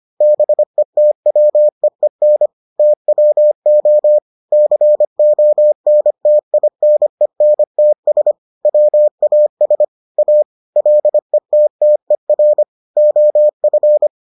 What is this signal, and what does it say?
Morse code, heard as a keyed tone, about 25 wpm, 600 Hz. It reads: BETWEEN TWO CONTINENTS WAS A LETTER OF